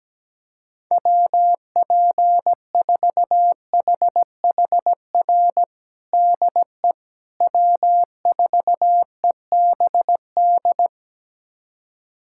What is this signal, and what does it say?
Morse code, heard as a keyed tone, about 17 wpm, 700 Hz. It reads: WP4HHR DE W4EBD
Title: USA Callsign DE Callsign 17wpm